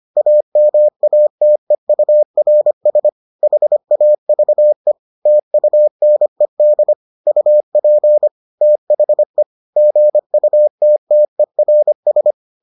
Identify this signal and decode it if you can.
Morse code, heard as a keyed tone, about 25 wpm, 600 Hz. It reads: AMATEURS HAVE TUNED UP THE GUTTERS